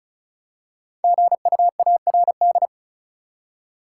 Morse code, heard as a keyed tone, about 34 wpm, 700 Hz. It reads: GUARD
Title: Five Letter Words - 35wpm